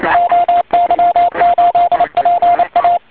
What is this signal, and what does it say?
Morse code, heard as a keyed tone, about 26 wpm, 690 Hz. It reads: OY8PA
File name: oy 2003 / real-audio / 160m-oy8pa-cw-10.5min